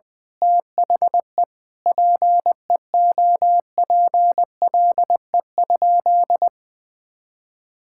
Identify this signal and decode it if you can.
Morse code, heard as a keyed tone, about 20 wpm, 700 Hz. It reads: THE PEOPLE?
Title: Sentences from Top 1000 Words - Review Part 5 - 20wpm